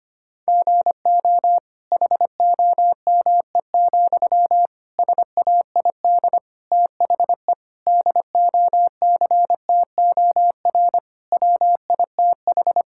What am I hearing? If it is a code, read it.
Morse code, heard as a keyed tone, about 25 wpm, 700 Hz. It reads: GO HOME, SAID THE DOCTOR WITH
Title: Sentences from Top 1600 Words 25wpm